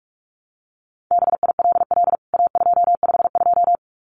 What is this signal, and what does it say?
Morse code, heard as a keyed tone, about 44 wpm, 700 Hz. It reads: BILL A252
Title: CWT Exchange 45wpm